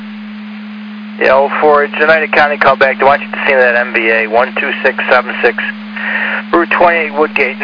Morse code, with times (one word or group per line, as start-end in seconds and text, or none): none